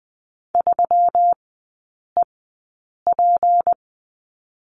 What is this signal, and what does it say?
Morse code, heard as a keyed tone, about 20 wpm, spaced as for 8 wpm, 700 Hz. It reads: 3EP
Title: 178 Lesson 29e - Slash - Three Characters 20wpm